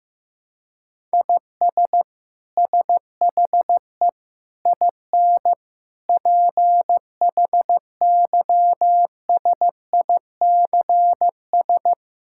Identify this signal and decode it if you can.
Morse code, heard as a keyed tone, about 15 wpm, 700 Hz. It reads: IS SHE IN PHYSICS
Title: Sentences from Top 1000 Words - Review Part 5 - 15wpm